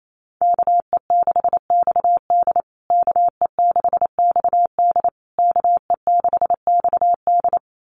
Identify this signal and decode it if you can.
Morse code, heard as a keyed tone, about 28 wpm, 700 Hz. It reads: KE6XD KE6XD KE6XD
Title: USA Call Signs Repeated 3x 28wpm